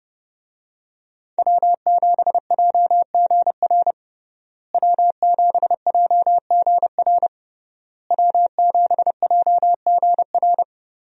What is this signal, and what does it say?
Morse code, heard as a keyed tone, about 32 wpm, 700 Hz. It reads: W7JGR W7JGR W7JGR
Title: USA Call Signs - 5 Characters - Repeated 3x - 30wpm